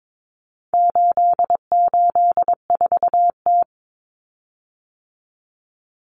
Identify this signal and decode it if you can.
Morse code, heard as a keyed tone, about 22 wpm, 700 Hz. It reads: 884T
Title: Common Cut Numbers - 4-Digit 22wpm